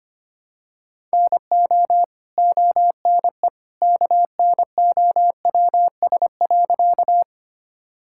Morse code, heard as a keyed tone, about 25 wpm, 700 Hz. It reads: NO ONE KNOWS.
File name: practice / 25wpm / Sentences with 2-Word Phrases - 25wpm